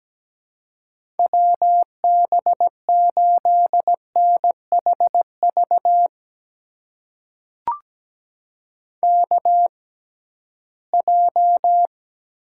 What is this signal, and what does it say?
Morse code, heard as a keyed tone, about 17 wpm, spaced as for 13 wpm, 700 Hz. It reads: WB8NHV KJ
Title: USA Call Signs - 6 Characters - 6x Character Spacing - 17wpm